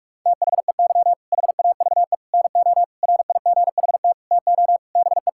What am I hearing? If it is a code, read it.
Morse code, heard as a keyed tone, about 46 wpm, 700 Hz. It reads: THEY HAVE NO RIGHT TO BE
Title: Sentences from Top 1000 Words - Review Part 6 - 45wpm